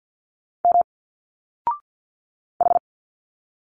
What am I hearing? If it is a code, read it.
Morse code, heard as a keyed tone, about 50 wpm, 700 Hz. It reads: M H
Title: Single Letters 50wpm